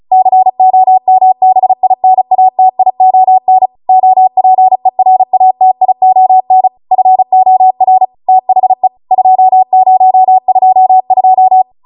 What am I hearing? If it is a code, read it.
Morse code, heard as a keyed tone, about 36 wpm, 750 Hz. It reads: COMBINATION OPERATION FOR THE 2022